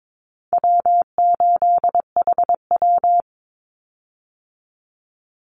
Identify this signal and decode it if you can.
Morse code, heard as a keyed tone, about 22 wpm, 700 Hz. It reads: W8HW
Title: Easy Call Signs Repeated 2x - 22wpm